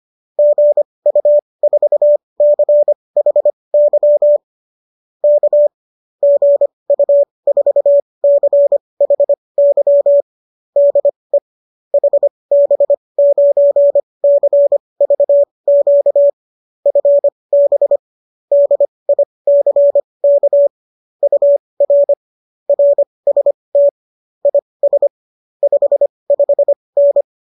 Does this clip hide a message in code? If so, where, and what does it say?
Yes: Morse code from 0.4 to 27.2 s, heard as a keyed tone, about 25 wpm, spaced as for 20 wpm, 570 Hz. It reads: GU4CHY K GU4CHY DE HB9CVQ FB DICK UR RST IS 55N